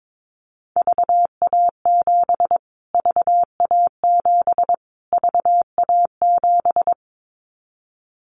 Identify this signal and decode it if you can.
Morse code, heard as a keyed tone, about 22 wpm, 700 Hz. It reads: VA7 VA7 VA7
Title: Callsign Prefixes for Top 25 DXCC - Repeated 3x - 22wpm